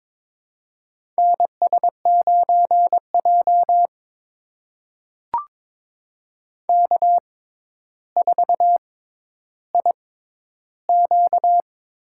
Morse code, heard as a keyed tone, about 22 wpm, spaced as for 11 wpm, 700 Hz. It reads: NS9J K4IQ
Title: USA Call Signs - 6x Character Spacing - 22wpm